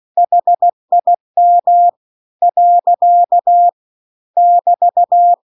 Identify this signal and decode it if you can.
Morse code, heard as a keyed tone, about 16 wpm, 700 Hz. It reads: HIM . <BT>
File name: CW-The-Big-Twoheaded-River-2-16WPM